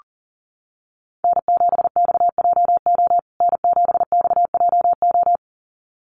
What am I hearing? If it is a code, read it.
Morse code, heard as a keyed tone, about 40 wpm, 700 Hz. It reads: N7XJO N7XJO
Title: USA Call Signs Repeated 2x 40wpm